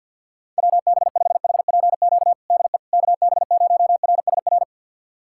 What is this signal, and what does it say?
Morse code, heard as a keyed tone, about 51 wpm, 700 Hz. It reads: WB5HPQ DE KB0RSR
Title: USA Callsign DE Callsign - 6 Characters - 50wpm